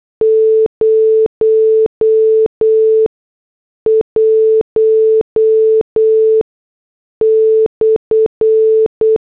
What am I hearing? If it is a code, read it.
Morse code, heard as a keyed tone, about 8 wpm, spaced as for 6 wpm, 440 Hz. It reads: 01/